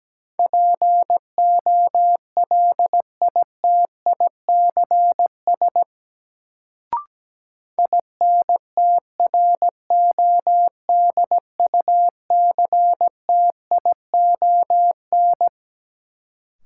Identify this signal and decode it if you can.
Morse code, heard as a keyed tone, about 17 wpm, 700 Hz. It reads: POLITICS INTRODUCTION
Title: Sentences from Top 1000 Words - New Word Drill - 17wpm